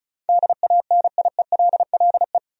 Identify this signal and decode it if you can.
Morse code, heard as a keyed tone, about 34 wpm, 700 Hz. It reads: DANIELLE
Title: USA Common Names - 35wpm